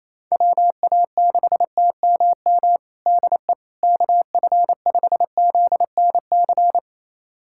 Morse code, heard as a keyed tone, about 28 wpm, 700 Hz. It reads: WA6TMM DE KF5ZNC
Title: USA Callsign DE Callsign - 6 Characters - 28wpm